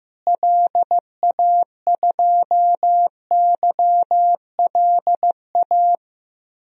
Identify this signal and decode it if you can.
Morse code, heard as a keyed tone, about 14 wpm, 700 Hz. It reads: LA2YLA
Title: DXCC-100 Call Signs - 6 Characters - Repeated 3x - 15wpm